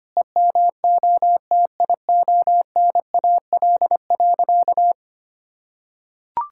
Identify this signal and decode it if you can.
Morse code, heard as a keyed tone, about 25 wpm, 700 Hz. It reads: EMOTIONAL.
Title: Sentences from Top 1000 Words - Review Part 5 - 25wpm